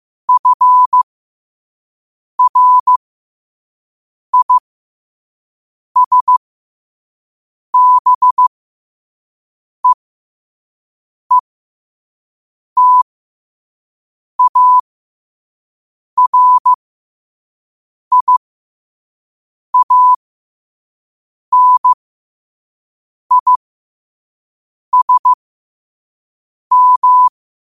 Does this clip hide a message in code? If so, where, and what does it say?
Yes: Morse code from 0.3 to 27.3 s, heard as a keyed tone, about 15 wpm, spaced as for 5 wpm, 1000 Hz. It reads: FRISBEETARIANISM